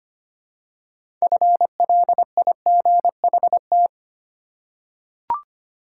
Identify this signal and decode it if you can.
Morse code, heard as a keyed tone, about 25 wpm, 700 Hz. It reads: FLIGHT
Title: Six Letter Words - 25wpm